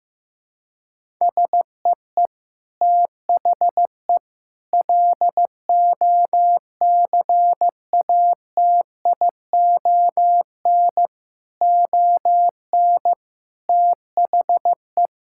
Morse code, heard as a keyed tone, about 16 wpm, 700 Hz. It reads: SEE THE LOCATION ON THE
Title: Sentences from Top 1000 Words - Review Part 6 - 15wpm